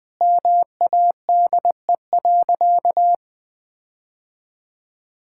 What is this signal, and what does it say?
Morse code, heard as a keyed tone, about 20 wpm, 700 Hz. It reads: MADE.
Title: Sentences with 5-Word Phrases - Spoken Beginning - 20wpm